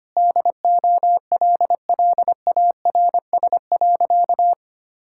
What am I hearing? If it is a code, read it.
Morse code, heard as a keyed tone, about 25 wpm, 700 Hz. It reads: DOLLARS.
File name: practice / 25wpm / Sentences with 9-Word Phrases - 25wpm